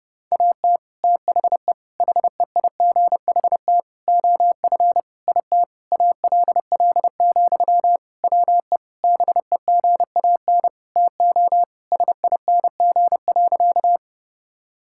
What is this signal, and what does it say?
Morse code, heard as a keyed tone, about 30 wpm, 700 Hz. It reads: AT THE HEIGHT OF IT ALL, WE BEGAN TO SING.
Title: Sentences from Top 1600 Words 30wpm